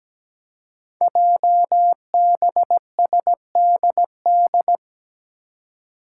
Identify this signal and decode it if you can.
Morse code, heard as a keyed tone, about 17 wpm, 700 Hz. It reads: JBSDD
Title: Five Challenging Letter-Number Combinations 17wpm